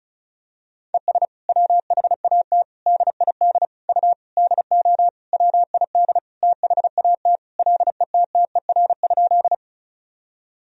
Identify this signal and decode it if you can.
Morse code, heard as a keyed tone, about 36 wpm, 700 Hz. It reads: ES WHAT DID U DO WID THAT LETTER?
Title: Sentences from Top 300 Words and Abbreviations 35wpm